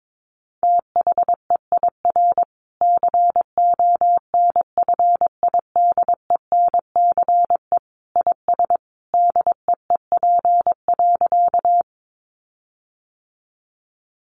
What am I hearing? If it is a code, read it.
Morse code, heard as a keyed tone, about 22 wpm, 700 Hz. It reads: THEIR CONFIDENCE IS DEEP.
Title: Sentences from Top 1500 Words 22wpm